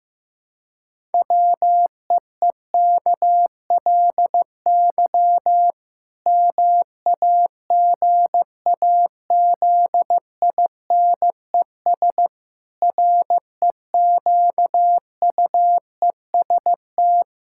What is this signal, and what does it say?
Morse code, heard as a keyed tone, about 15 wpm, 700 Hz. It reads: WEEKLY MAGAZINES REQUEST
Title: Pangram Challenge v2 - 15wpm